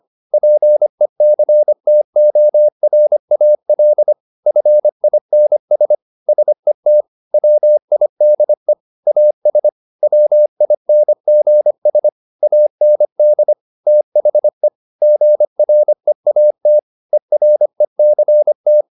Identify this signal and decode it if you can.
Morse code, heard as a keyed tone, about 25 wpm, 600 Hz. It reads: PECTORAL FINS SET WIDE AS WINGS AND THE GREAT ERECT
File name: the old man and the sea part 1 25wpm